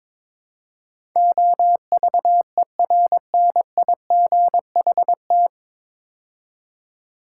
Morse code, heard as a keyed tone, about 22 wpm, 700 Hz. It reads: OVERNIGHT
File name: 9 Letter Words - First 3 Letters Given - 22wpm